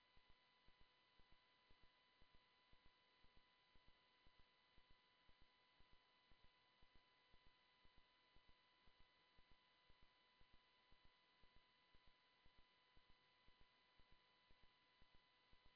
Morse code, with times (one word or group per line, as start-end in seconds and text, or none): none